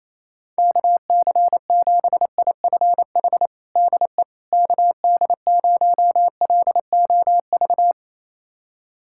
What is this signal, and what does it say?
Morse code, heard as a keyed tone, about 28 wpm, 700 Hz. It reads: KC7IFH DE KD0LOV
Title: USA Callsign DE Callsign 28wpm